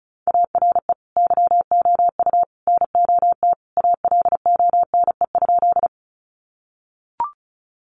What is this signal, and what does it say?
Morse code, heard as a keyed tone, about 34 wpm, 700 Hz. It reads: ARE YOU NOT ALONE?
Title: Sentences from Top 1000 Words 35wpm